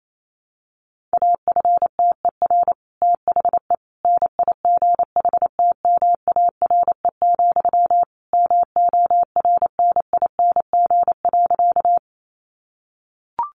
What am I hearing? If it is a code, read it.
Morse code, heard as a keyed tone, about 28 wpm, 700 Hz. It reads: AFTER THE NIGHTMARE, MORNING.
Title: A Sound Of Thunder 28wpm